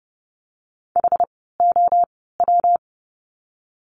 Morse code, heard as a keyed tone, about 30 wpm, spaced as for 17 wpm, 700 Hz. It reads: HOW
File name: course / 30:10wpm / 230 Lesson 37f - Letter x - Words 30wpm